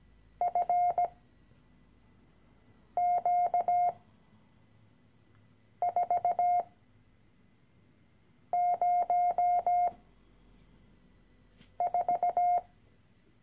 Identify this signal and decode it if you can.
Morse code, heard as a keyed tone, about 17 wpm, spaced as for 4 wpm, 710 Hz. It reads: FQ404
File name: CW-Online-Kurs / Anfänger / Lektion 013